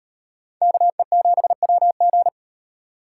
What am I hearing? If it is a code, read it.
Morse code, heard as a keyed tone, about 36 wpm, 700 Hz. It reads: KE7WG